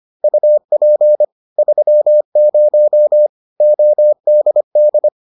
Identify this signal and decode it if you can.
Morse code, heard as a keyed tone, about 25 wpm, 600 Hz. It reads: UP 30 ODD